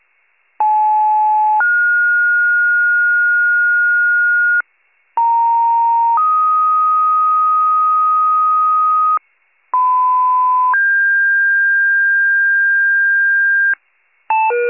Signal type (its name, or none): two-tone page